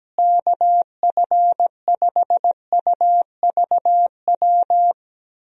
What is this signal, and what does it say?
Morse code, heard as a keyed tone, about 17 wpm, 700 Hz. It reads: KF5UVW
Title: USA Call Signs - 6 Characters - Repeated 2x - 17wpm